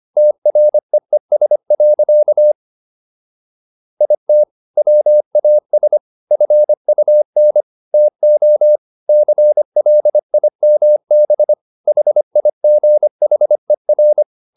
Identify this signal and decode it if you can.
Morse code, heard as a keyed tone, about 25 wpm, 600 Hz. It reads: TREES. IT WAS FUN TO CLIMB HIGHER